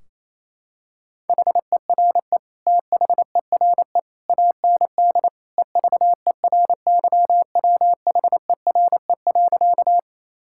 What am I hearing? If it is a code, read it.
Morse code, heard as a keyed tone, about 27 wpm, 700 Hz. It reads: HERE THERE AND EVERYWHERE.
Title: Sentences from Top 1600 Words 28wpm